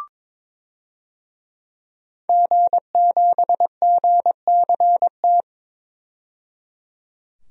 Morse code, heard as a keyed tone, about 22 wpm, 700 Hz. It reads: G7GCT
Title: DXCC-100 Call Signs - 5 Characters - 22wpm